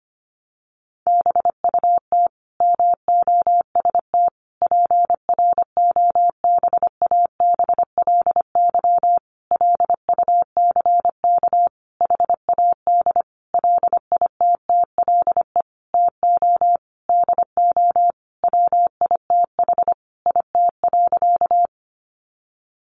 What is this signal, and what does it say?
Morse code, heard as a keyed tone, about 25 wpm, 700 Hz. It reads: BUT MOST PROBABLY LUCK HAD LITTLE TO DO WITH IT.
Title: Sentences from Top 1500 Words 25wpm